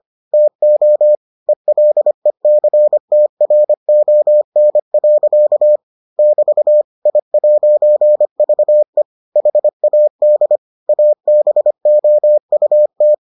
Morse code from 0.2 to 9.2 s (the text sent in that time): TO ELECTRON. <BT> I'VE